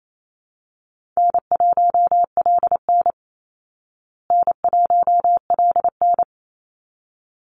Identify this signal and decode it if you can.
Morse code, heard as a keyed tone, about 28 wpm, 700 Hz. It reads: N1LN N1LN
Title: Easy Call Signs Repeated 2x - 28wpm